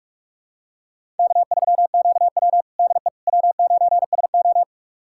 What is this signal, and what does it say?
Morse code, heard as a keyed tone, about 44 wpm, 700 Hz. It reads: K3QW DE W9SO
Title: USA Callsign DE Callsign - 4 Characters - 45wpm